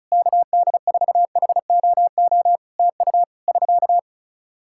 Morse code, heard as a keyed tone, about 34 wpm, 700 Hz. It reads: KD4HOO TU <SK>
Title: Practice QSO - Exchange 4 - Ending - 35wpm